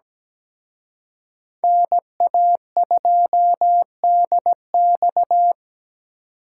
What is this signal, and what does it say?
Morse code, heard as a keyed tone, about 20 wpm, 700 Hz. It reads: NA2DX
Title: Speed Racing - USA Call Signs 17wpm